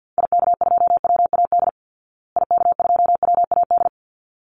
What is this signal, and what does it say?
Morse code, heard as a keyed tone, about 48 wpm, 700 Hz. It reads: IK2WAD IK2WAD
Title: Call Signs Repeated 3x - 50wpm